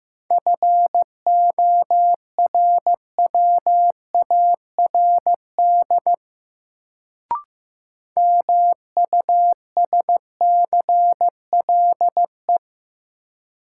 Morse code, heard as a keyed tone, about 15 wpm, 700 Hz. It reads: FORWARD MUSCLE
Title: Sentences from Top 1300 Words - New Word Drill - 15wpm